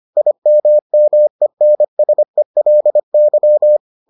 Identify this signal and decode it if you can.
Morse code, heard as a keyed tone, about 25 wpm, 600 Hz. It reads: IMMENSELY